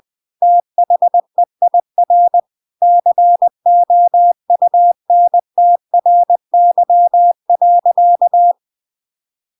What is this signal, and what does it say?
Morse code, heard as a keyed tone, about 20 wpm, 700 Hz. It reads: THEIR COUNTRY.
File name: Now is the time - 20wpm